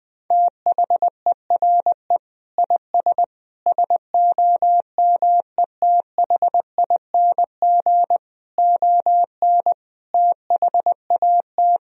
Morse code, heard as a keyed tone, about 21 wpm, 700 Hz. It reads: THERE IS SOMETHING ON THAT
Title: Sentences from Top 1000 Words - Review Part 3 - 20wpm